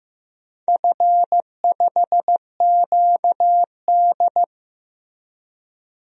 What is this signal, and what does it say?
Morse code, heard as a keyed tone, about 15 wpm, 700 Hz. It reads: F5QD
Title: DXCC-100 Call Signs - 4 Characters - 15wpm